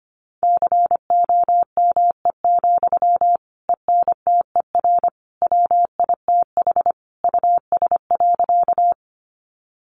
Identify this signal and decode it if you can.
Morse code, heard as a keyed tone, about 25 wpm, 700 Hz. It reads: COME, ENTER WITH US.
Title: Sentences from Top 1300 Words 25wpm